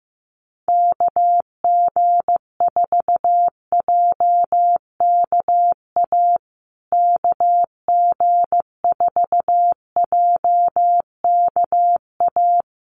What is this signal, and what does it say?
Morse code, heard as a keyed tone, about 15 wpm, 700 Hz. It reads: KG4JKA KG4JKA
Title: USA Call Signs Repeated 2x 15wpm